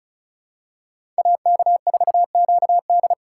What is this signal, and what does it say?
Morse code, heard as a keyed tone, about 36 wpm, 700 Hz. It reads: AK4QD
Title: USA Call Signs - 5 Characters - 4x Character Spacing - 35wpm